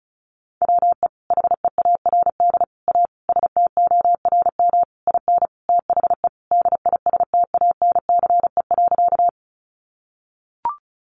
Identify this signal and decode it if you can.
Morse code, heard as a keyed tone, about 34 wpm, 700 Hz. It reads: WE HEARD A STORM IN THE DISTANCE.
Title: Sentences from Top 1500 Words 35wpm